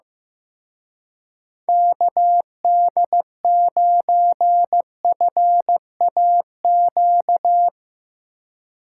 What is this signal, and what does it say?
Morse code, heard as a keyed tone, about 16 wpm, 700 Hz. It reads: KD9FAQ
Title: Speed Racing - USA Call Signs 15wpm